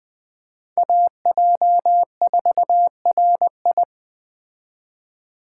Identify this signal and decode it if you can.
Morse code, heard as a keyed tone, about 20 wpm, 700 Hz. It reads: AJ4RI